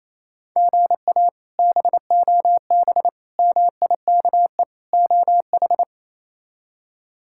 Morse code, heard as a keyed tone, about 28 wpm, 700 Hz. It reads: GA BOB MIKE OH
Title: K1USN SST QSO Practice - 28wpm